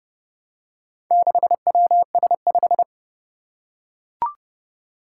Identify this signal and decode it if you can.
Morse code, heard as a keyed tone, about 31 wpm, 700 Hz. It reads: 6WS5
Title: Four Challenging Letter-Number Combinations 30wpm